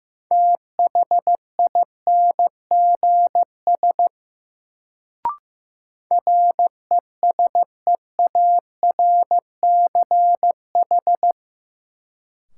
Morse code, heard as a keyed tone, about 15 wpm, 700 Hz. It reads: THINGS RESEARCH
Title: Sentences from Top 1000 Words - Review Part 4 - New Word Drill - 15wpm